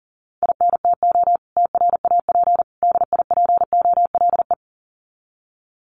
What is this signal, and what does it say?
Morse code, heard as a keyed tone, about 40 wpm, 700 Hz. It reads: INTO TRAP DIPOLE
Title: Common QSO Phrases 40wpm